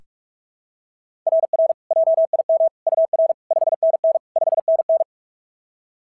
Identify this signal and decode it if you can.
Morse code, heard as a keyed tone, about 38 wpm, 660 Hz. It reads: RR JIM UR 5NN 5NN
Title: SOTA QSO Practice 45wpm